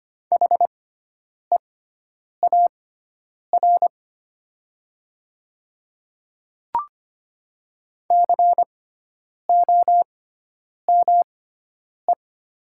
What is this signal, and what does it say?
Morse code, heard as a keyed tone, about 25 wpm, spaced as for 9 wpm, 700 Hz. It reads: HEAR COME
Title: 6x Character Spacing - Top 300 Words - 25wpm